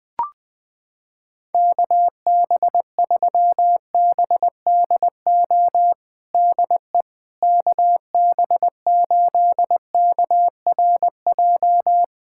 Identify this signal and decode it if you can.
Morse code, heard as a keyed tone, about 21 wpm, 700 Hz. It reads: KB3BDO DE KB8KRJ
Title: USA Callsign DE Callsign 20wpm